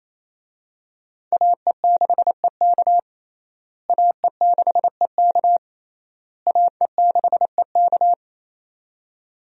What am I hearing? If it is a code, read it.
Morse code, heard as a keyed tone, about 27 wpm, 700 Hz. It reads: AE6EK AE6EK AE6EK
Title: USA Call Signs - 5 Characters - Repeated 3x - 28wpm